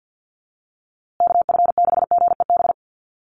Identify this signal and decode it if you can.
Morse code, heard as a keyed tone, about 48 wpm, 700 Hz. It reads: KF6GEB
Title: USA Call Signs Repeated 3x 50wpm